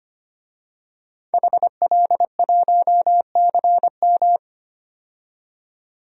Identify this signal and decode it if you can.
Morse code, heard as a keyed tone, about 25 wpm, 700 Hz. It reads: HL1CM
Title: DXCC-100 Call Signs - 5 Characters - 25wpm